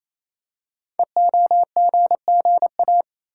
Morse code, heard as a keyed tone, about 27 wpm, 700 Hz. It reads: EOGGA
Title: Code Groups - 5 Letters - 28wpm